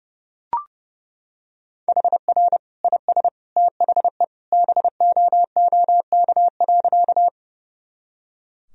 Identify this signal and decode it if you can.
Morse code, heard as a keyed tone, about 30 wpm, 700 Hz. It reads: HR IS THE BOOK.
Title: Sentences from Top 300 Words and Abbreviations 30wpm